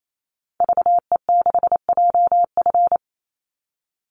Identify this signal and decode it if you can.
Morse code, heard as a keyed tone, about 28 wpm, 700 Hz. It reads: VE6JF
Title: Call Signs - 28wpm